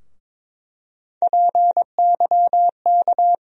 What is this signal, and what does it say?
Morse code, heard as a keyed tone, about 21 wpm, 700 Hz. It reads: PYK
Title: USA Call Sign Postfixes - 3 Letters 22wpm